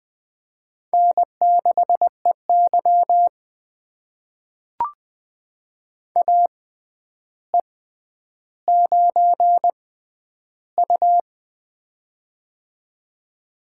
Morse code, heard as a keyed tone, about 20 wpm, spaced as for 10 wpm, 700 Hz. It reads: N6EY AE9U
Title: USA Call Signs - 4 Characters - 6x Character Spacing - 20wpm